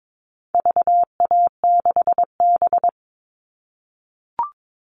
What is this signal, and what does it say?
Morse code, heard as a keyed tone, about 22 wpm, 700 Hz. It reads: VA6B
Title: All Cut Numbers - 4-Digit 22wpm